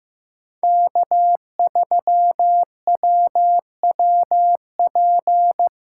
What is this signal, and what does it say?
Morse code, heard as a keyed tone, about 15 wpm, 700 Hz. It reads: K3WWP
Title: K1USN SST QSO Practice - 15wpm